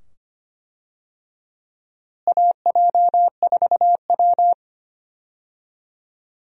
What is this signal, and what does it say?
Morse code, heard as a keyed tone, about 25 wpm, 700 Hz. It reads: AJ4W
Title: USA Call Signs - 4 Characters - Repeated 2x - 25wpm